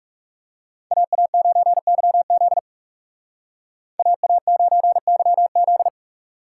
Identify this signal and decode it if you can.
Morse code, heard as a keyed tone, about 40 wpm, 700 Hz. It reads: AA9YZ AA9YZ